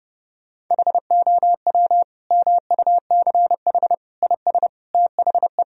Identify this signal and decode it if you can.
Morse code, heard as a keyed tone, about 30 wpm, 700 Hz. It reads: HOW MUCH IS THE
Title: Sentences from Top 1000 Words - Review Part 1 - 30wpm